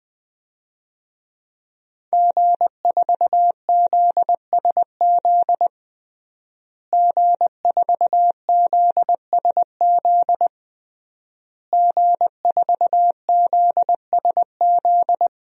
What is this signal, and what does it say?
Morse code, heard as a keyed tone, about 20 wpm, 700 Hz. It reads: G4ZSZ G4ZSZ G4ZSZ
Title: DXCC-100 Call Signs - 5 Characters - Repeated 3x - 20wpm